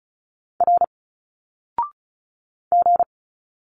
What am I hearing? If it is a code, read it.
Morse code, heard as a keyed tone, about 36 wpm, 700 Hz. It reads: R G
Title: Single Letters 35wpm